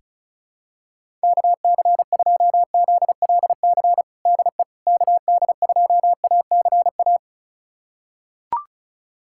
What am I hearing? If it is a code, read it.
Morse code, heard as a keyed tone, about 36 wpm, 700 Hz. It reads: KC2ZLC DE KD2ACA
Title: USA Callsign DE Callsign 35wpm